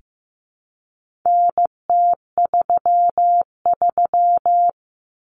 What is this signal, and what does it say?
Morse code, heard as a keyed tone, about 15 wpm, 700 Hz. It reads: NT33
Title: Common Cut Numbers - 4-Digit 15wpm